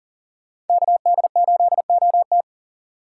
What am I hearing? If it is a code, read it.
Morse code, heard as a keyed tone, about 40 wpm, 700 Hz. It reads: KD8OT
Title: USA Call Signs - 5 Characters - 2x Character Spacing - 40wpm